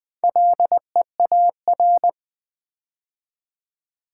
Morse code, heard as a keyed tone, about 20 wpm, 700 Hz. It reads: LEAR
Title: Build A Word - 2 Letters At A Time - 8-Letter Words - 20wpm